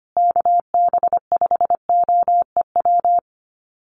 Morse code, heard as a keyed tone, about 25 wpm, 700 Hz. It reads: KB5OEW
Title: USA Call Signs Repeated 3x 25wpm